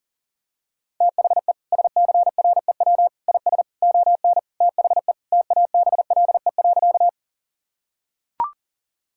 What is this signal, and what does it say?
Morse code, heard as a keyed tone, about 37 wpm, 700 Hz. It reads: THE SCREW IS ON THE TABLE.
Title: Sentences from Top 1600 Words 40wpm